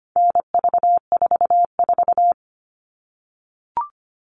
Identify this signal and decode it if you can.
Morse code, heard as a keyed tone, about 25 wpm, 700 Hz. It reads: NV44
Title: All Cut Numbers - 4-Digit 25wpm